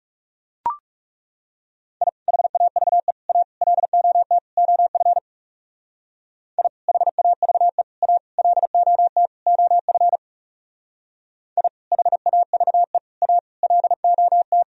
Morse code, heard as a keyed tone, about 36 wpm, 700 Hz. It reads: I HAVE A LOT OF I HAVE A LOT OF I HAVE A LOT
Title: Speed-Racing - Sets of 5 Words 30wpm